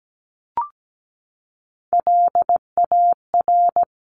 Morse code, heard as a keyed tone, about 17 wpm, 700 Hz. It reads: LAR
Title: CWT Exchange 17wpm